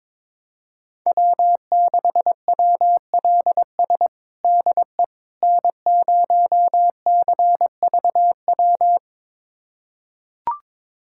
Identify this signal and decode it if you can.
Morse code, heard as a keyed tone, about 22 wpm, 700 Hz. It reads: W6WLS DE N0CVW
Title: USA Callsign DE Callsign - 5 Characters - 22wpm